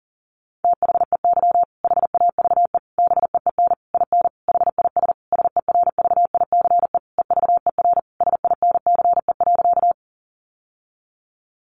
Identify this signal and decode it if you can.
Morse code, heard as a keyed tone, about 40 wpm, 700 Hz. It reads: THEY HAVE BEEN IN HIS SERVICE EVER SINCE.
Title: Sentences from Top 700 Words 40wpm